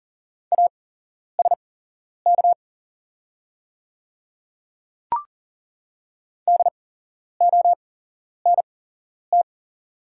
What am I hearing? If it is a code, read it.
Morse code, heard as a keyed tone, about 40 wpm, spaced as for 11 wpm, 690 Hz. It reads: ASK DONT
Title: 8x Character Spacing - Top 300 Words - 40wpm